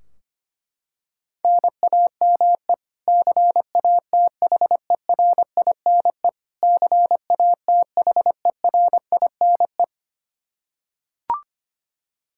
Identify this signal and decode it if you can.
Morse code, heard as a keyed tone, about 25 wpm, 700 Hz. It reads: NAME CATHERINE CATHERINE
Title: Practice QSO - Exchange 1 - Beginning - 25wpm